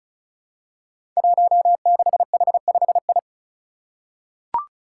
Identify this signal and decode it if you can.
Morse code, heard as a keyed tone, about 34 wpm, 700 Hz. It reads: 16H5I
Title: Five Challenging Letter-Number Combinations 35wpm